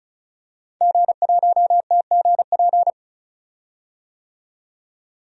Morse code, heard as a keyed tone, about 36 wpm, 700 Hz. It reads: G1TGP
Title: DXCC-100 Call Signs - 5 Characters - 35wpm